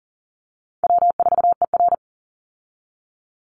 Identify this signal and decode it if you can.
Morse code, heard as a keyed tone, about 40 wpm, 700 Hz. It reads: W4ER
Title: Easy Call Signs Repeated 2x - 40wpm